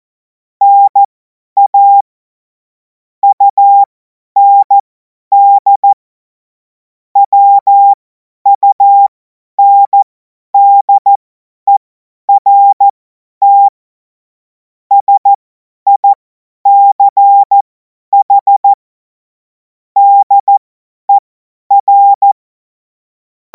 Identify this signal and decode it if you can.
Morse code, heard as a keyed tone, about 14 wpm, spaced as for 10 wpm, 800 Hz. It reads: NA UND WUNDERT SICH DER
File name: Morsekurs / Witze / Witz 04. August 2026 50 F70